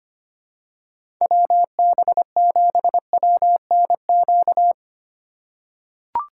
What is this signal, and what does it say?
Morse code, heard as a keyed tone, about 25 wpm, 700 Hz. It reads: WB7WNQ
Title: USA Call Signs - 2x Character Spacing - 25wpm